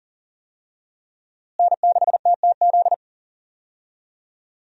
Morse code, heard as a keyed tone, about 38 wpm, 700 Hz. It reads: N6TTZ
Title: USA Call Signs - 5 Characters - Repeated 3x - 40wpm